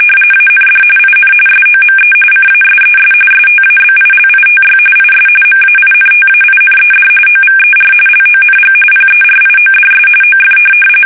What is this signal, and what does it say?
RTTY, about 45.45 baud. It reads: AND IN SOME CASES STABILITY IS MAINTAINED OVER LONG PERIODS OF